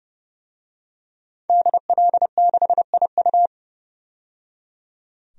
Morse code, heard as a keyed tone, about 30 wpm, 700 Hz. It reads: DL6IU
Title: DXCC-100 Call Signs - 5 Characters - 30wpm